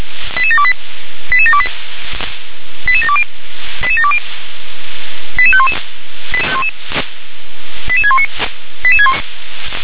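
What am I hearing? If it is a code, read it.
Five-tone selective call (ZVEI), heard as quick runs of five tones, tones of about 70 ms. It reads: 88728 88728 88720 88720 88511 88619
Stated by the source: ZVEI five-tone selective call received with an SDR.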